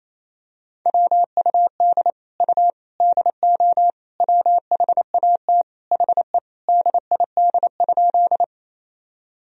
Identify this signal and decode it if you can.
Morse code, heard as a keyed tone, about 28 wpm, 700 Hz. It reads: WUD U DO WHAT HE DID?
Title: Sentences from Top 100 Words and Abbreviations 28wpm